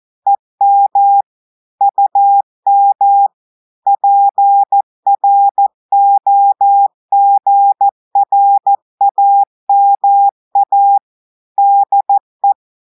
Morse code, heard as a keyed tone, about 14 wpm, 800 Hz. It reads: EM UM PROGRAMA DE
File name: audios / portugais / portugais texte en clair 2 vitesse 14 mots minutes